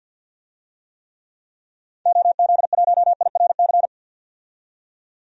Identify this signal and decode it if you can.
Morse code, heard as a keyed tone, about 51 wpm, 700 Hz. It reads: OZ1IRC
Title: DXCC-100 Call Signs - 6 Characters - 50wpm